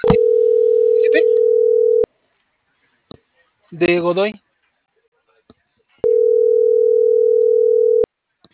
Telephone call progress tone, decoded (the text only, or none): ringback tone